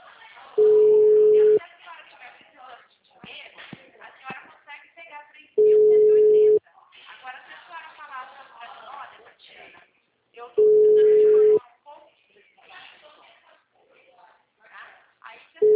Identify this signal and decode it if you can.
Ringback tone (Europe, 425 Hz), with speech.